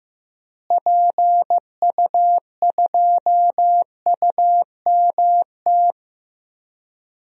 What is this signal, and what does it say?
Morse code, heard as a keyed tone, about 15 wpm, 700 Hz. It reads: PU2UMT
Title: DXCC-100 Call Signs - 6 Characters - Repeated 3x - 15wpm